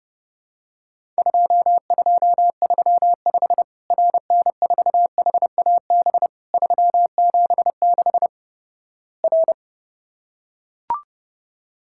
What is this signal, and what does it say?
Morse code, heard as a keyed tone, about 32 wpm, 700 Hz. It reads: 2235 RN4HAB 376 R
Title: Worked All Europe DX Contest - QTC Exchange - 30wpm